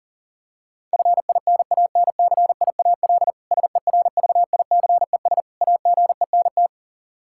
Morse code, heard as a keyed tone, about 37 wpm, 700 Hz. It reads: FINANCIAL SERVICES AGENT
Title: Practice QSO - Exchange 3 - More Info - 40wpm